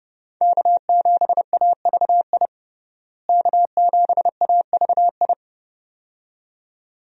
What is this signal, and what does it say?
Morse code, heard as a keyed tone, about 30 wpm, 700 Hz. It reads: K7AVI K7AVI
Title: USA Call Signs - 5 Characters - Repeated 2x - 30wpm